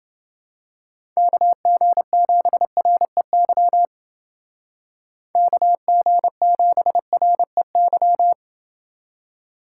Morse code, heard as a keyed tone, about 28 wpm, 700 Hz. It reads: KG7REY KG7REY